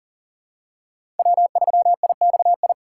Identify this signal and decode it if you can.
Morse code, heard as a keyed tone, about 40 wpm, 700 Hz. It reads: W3IXI
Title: USA Call Signs - 3x Character Spacing - 40wpm